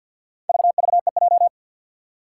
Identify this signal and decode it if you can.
Morse code, heard as a keyed tone, about 48 wpm, 700 Hz. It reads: VVEJ
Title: Four Challenging Letter-Number Combinations 50wpm